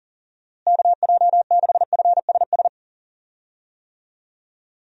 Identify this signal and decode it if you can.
Morse code, heard as a keyed tone, about 37 wpm, 700 Hz. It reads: KJ6FSS
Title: USA Call Signs - 6 Characters - Repeated 2x - 40wpm